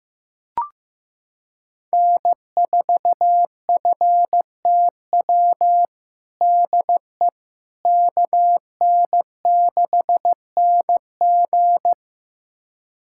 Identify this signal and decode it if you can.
Morse code, heard as a keyed tone, about 15 wpm, 700 Hz. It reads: N4FTW DE KN6NG
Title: USA Callsign DE Callsign - 5 Characters - 15wpm